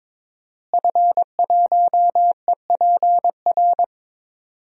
Morse code, heard as a keyed tone, about 22 wpm, 700 Hz. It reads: F1EPR
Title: DXCC-100 Call Signs - 5 Characters - Repeated 3x - 22wpm